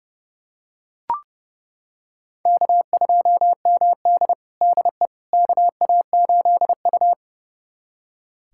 Morse code, heard as a keyed tone, about 30 wpm, 700 Hz. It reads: K2MD DE KA8U
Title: USA Callsign DE Callsign - 4 Characters - 30wpm